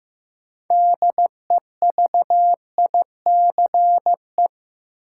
Morse code, heard as a keyed tone, about 15 wpm, 700 Hz. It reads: DEVICE
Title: Sentences from Top 1500 Words - New Word Drill - 15wpm